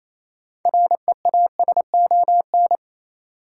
Morse code, heard as a keyed tone, about 28 wpm, 700 Hz. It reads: REASON
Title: Sets of 5 Words Encore - New Word Drill - 28wpm